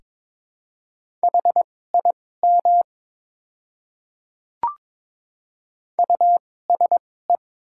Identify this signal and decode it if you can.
Morse code, heard as a keyed tone, about 22 wpm, spaced as for 16 wpm, 700 Hz. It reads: HIM USE
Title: 2x Character Spacing - Top 100 Words - 22wpm